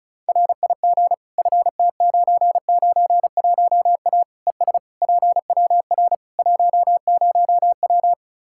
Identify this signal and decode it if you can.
Morse code, heard as a keyed tone, about 36 wpm, 700 Hz. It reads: RIG FT991A ES PWR 10W
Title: Practice QSO - Exchange 2 - Info - 35wpm